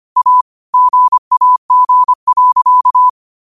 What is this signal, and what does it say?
Morse code, heard as a keyed tone, about 25 wpm, 1000 Hz. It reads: A GAG.